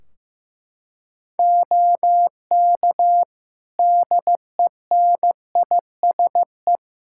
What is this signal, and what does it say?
Morse code, heard as a keyed tone, about 15 wpm, 700 Hz. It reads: OK DENISE
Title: K1USN SST QSO Practice 15wpm